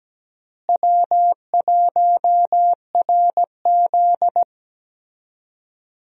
Morse code, heard as a keyed tone, about 17 wpm, 700 Hz. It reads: W1RZ
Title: USA Call Signs - 4 Characters - Repeated 2x - 17wpm